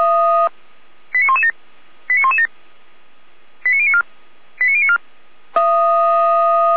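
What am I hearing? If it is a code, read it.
Five-tone selective call (ZVEI), heard as quick runs of five tones, tones of about 70 ms. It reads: 89117 89117 89094 89094
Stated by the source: ZVEI five-tone selective call received by SDR.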